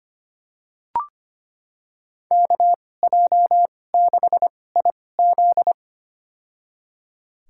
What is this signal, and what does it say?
Morse code, heard as a keyed tone, about 25 wpm, spaced as for 18 wpm, 700 Hz. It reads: KJ6IZ
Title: USA Call Signs - 5 Characters - 2x Character Spacing - 25wpm